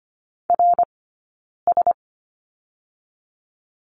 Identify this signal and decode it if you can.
Morse code, heard as a keyed tone, about 25 wpm, 700 Hz. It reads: R S
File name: Lesson 17d - Letter u - Two Characters 25wpm